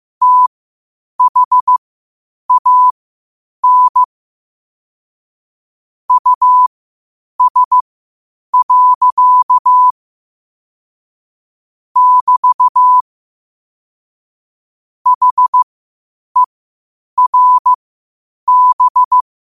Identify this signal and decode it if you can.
Morse code, heard as a keyed tone, about 15 wpm, spaced as for 8 wpm, 1000 Hz. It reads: THAN US. <BT> HERB
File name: qotdfiles / 8wpm